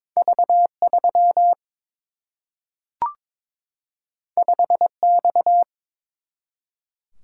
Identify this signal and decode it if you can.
Morse code, heard as a keyed tone, about 21 wpm, 700 Hz. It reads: V 3 5 X
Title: Single-Letter Single-Number 22wpm